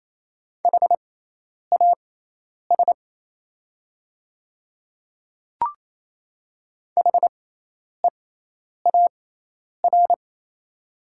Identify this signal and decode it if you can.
Morse code, heard as a keyed tone, about 28 wpm, spaced as for 10 wpm, 700 Hz. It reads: HAS HEAR
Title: 6x Character Spacing - Top 300 Words - 28wpm